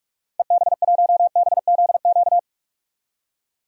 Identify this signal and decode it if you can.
Morse code, heard as a keyed tone, about 46 wpm, 700 Hz. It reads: EB1BZQ
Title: DXCC-100 Call Signs - 6 Characters - 45wpm